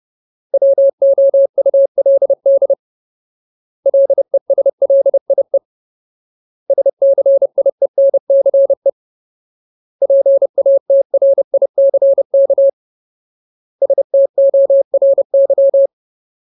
Morse code, heard as a keyed tone, about 30 wpm, 550 Hz. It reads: WOULD LESLIE SCIENCE PATRICK STORY
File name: Ultimate Word List / 30WPM Ultimate